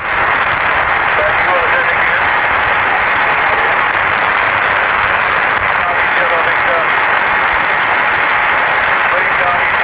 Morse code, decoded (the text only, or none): none